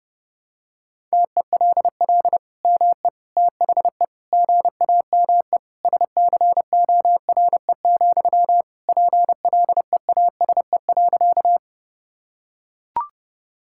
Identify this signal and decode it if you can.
Morse code, heard as a keyed tone, about 30 wpm, 700 Hz. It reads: TELL ME THE GAME SCORE, PLEASE.
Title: Sentences from Top 1600 Words 30wpm